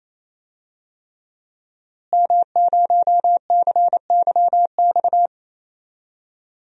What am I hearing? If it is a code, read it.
Morse code, heard as a keyed tone, about 28 wpm, 700 Hz. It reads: M0CYX
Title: DXCC-100 Call Signs - 5 Characters - Repeated 3x - 28wpm